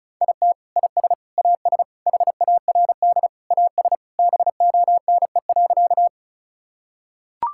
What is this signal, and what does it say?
Morse code, heard as a keyed tone, about 34 wpm, 700 Hz. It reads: IT IS AS HARD AS BONE.
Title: Sentences from Top 1600 Words 35wpm